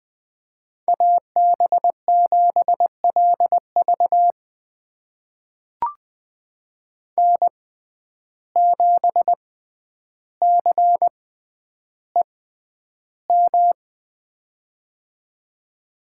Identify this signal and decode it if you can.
Morse code, heard as a keyed tone, about 20 wpm, spaced as for 10 wpm, 700 Hz. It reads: AB7LV N7CEM
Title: USA Call Signs - 5 Characters - 6x Character Spacing - 20wpm